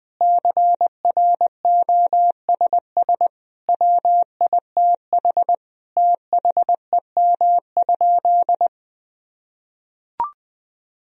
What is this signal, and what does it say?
Morse code, heard as a keyed tone, about 20 wpm, 700 Hz. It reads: CROSS WITH THEM?
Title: Sentences from Top 1000 Words - Review Part 1 - 20wpm